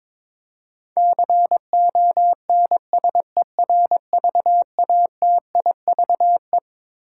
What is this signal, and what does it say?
Morse code, heard as a keyed tone, about 21 wpm, 700 Hz. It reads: CONSERVATIVE
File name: Twelve Letter Words - 22wpm